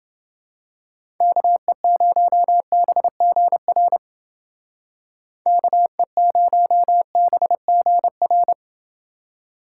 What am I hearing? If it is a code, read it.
Morse code, heard as a keyed tone, about 29 wpm, 700 Hz. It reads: KE0BGR KE0BGR